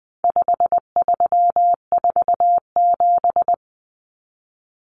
Morse code, heard as a keyed tone, about 20 wpm, 700 Hz. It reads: 5347
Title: Common Cut Numbers - 4-Digit 20wpm